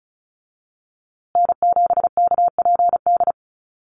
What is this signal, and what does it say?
Morse code, heard as a keyed tone, about 34 wpm, 700 Hz. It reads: N7KPD
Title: USA Call Signs Repeated 3x 35wpm